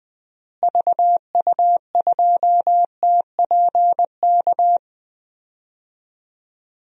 Morse code, heard as a keyed tone, about 20 wpm, 700 Hz. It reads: VU2TPK
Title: DXCC-100 Call Signs - 6 Characters - Repeated 3x - 20wpm